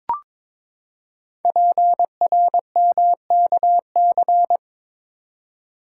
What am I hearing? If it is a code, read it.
Morse code, heard as a keyed tone, about 22 wpm, 700 Hz. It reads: PRMKC
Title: Code Groups - 5 Letters - 22wpm